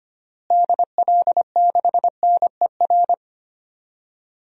Morse code, heard as a keyed tone, about 25 wpm, 700 Hz. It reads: DL6NER
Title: DXCC-100 Call Signs - 6 Characters - Repeated 3x - 25wpm